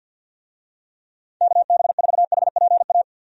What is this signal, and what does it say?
Morse code, heard as a keyed tone, about 50 wpm, 700 Hz. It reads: KB4HPA
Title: USA Call Signs - 6 Characters - 50wpm